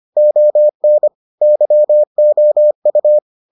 Morse code, heard as a keyed tone, about 25 wpm, 600 Hz. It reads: ON YOU